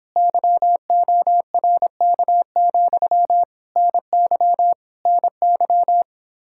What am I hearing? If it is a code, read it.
Morse code, heard as a keyed tone, about 26 wpm, 700 Hz. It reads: YORK, NY NY
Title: Speed-Racing Common QSO Phrases - 22wpm